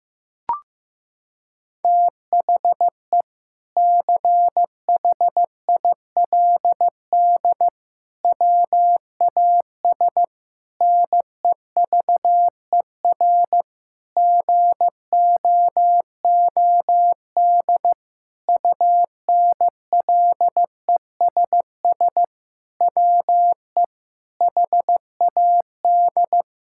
Morse code, heard as a keyed tone, about 15 wpm, 700 Hz. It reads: THE CHILD WAS NEVER GOOD UNLESS WE HAD
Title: Sentences from Top 1000 Words - Review Part 6 - 15wpm